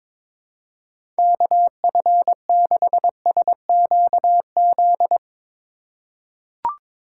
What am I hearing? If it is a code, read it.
Morse code, heard as a keyed tone, about 22 wpm, 700 Hz. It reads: KF6SQZ
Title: USA Call Signs - 6 Characters - 3x Character Spacing - 22wpm